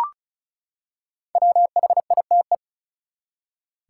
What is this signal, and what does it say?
Morse code, heard as a keyed tone, about 36 wpm, 700 Hz. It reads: WHITE
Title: Five Letter Words - 35wpm